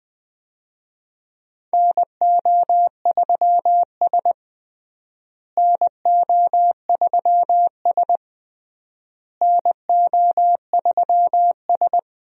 Morse code, heard as a keyed tone, about 20 wpm, 700 Hz. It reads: NO3S NO3S NO3S
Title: USA Call Signs - 4 Characters - Repeated 3x - 20wpm